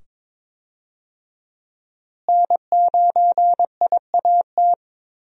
Morse code, heard as a keyed tone, about 22 wpm, 700 Hz. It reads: N9IAT
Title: USA Call Signs - 5 Characters - Repeated 2x - 22wpm